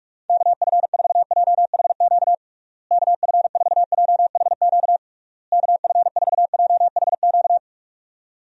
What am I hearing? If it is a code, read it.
Morse code, heard as a keyed tone, about 47 wpm, 700 Hz. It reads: KF4JHQ KF4JHQ KF4JHQ
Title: USA Call Signs - 6 Characters - Repeated 3x - 45wpm